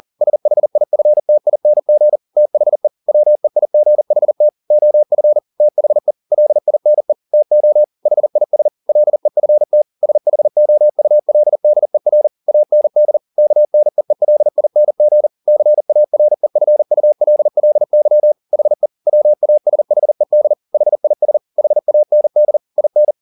Morse code, heard as a keyed tone, about 40 wpm, 600 Hz. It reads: SHIFTING THE WEIGHT OF THE LINE TO HIS LEFT SHOULDER AND KNEELING CAREFULLY HE WASHED HIS HAND IN